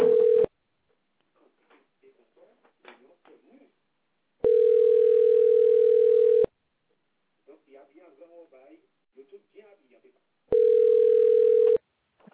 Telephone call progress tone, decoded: ringback tone